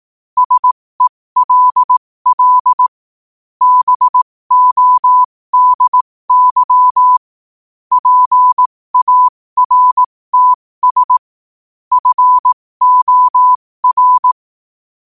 Morse code, heard as a keyed tone, about 18 wpm, spaced as for 15 wpm, 1000 Hz. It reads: SELL BODY PARTS FOR
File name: qotdfiles / 15wpm